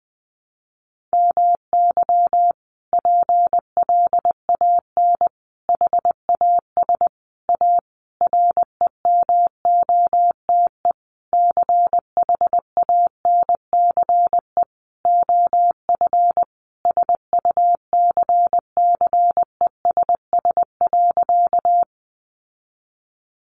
Morse code, heard as a keyed tone, about 20 wpm, 700 Hz. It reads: MY PLAN HAS A REMOTE CHANCE OF SUCCESS.
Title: Sentences from Top 1300 Words 20wpm